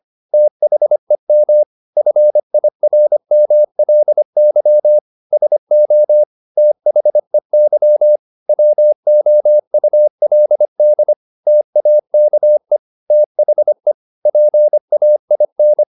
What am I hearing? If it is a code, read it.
Morse code, heard as a keyed tone, about 25 wpm, 600 Hz. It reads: THEM FIRMLY SO THEY WOULD TAKE THE PAIN